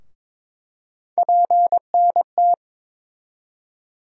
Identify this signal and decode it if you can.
Morse code, heard as a keyed tone, about 22 wpm, 700 Hz. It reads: PNT
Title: USA Call Sign Postfixes - 3 Letters 22wpm